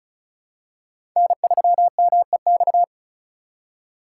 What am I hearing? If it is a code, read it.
Morse code, heard as a keyed tone, about 36 wpm, 700 Hz. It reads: N3MEX
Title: USA Call Signs - 5 Characters - 4x Character Spacing - 35wpm